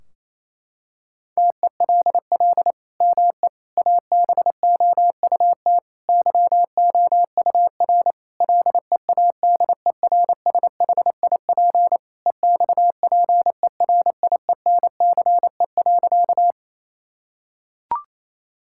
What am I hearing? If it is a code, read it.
Morse code, heard as a keyed tone, about 27 wpm, 700 Hz. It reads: TELL ME ABOUT YOUR LEADERSHIP EXPERIENCE.
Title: Sentences from Top 1600 Words 28wpm